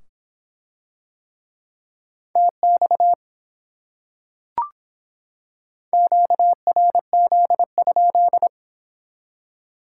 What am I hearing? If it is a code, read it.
Morse code, heard as a keyed tone, about 25 wpm, 700 Hz. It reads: TX QRZ?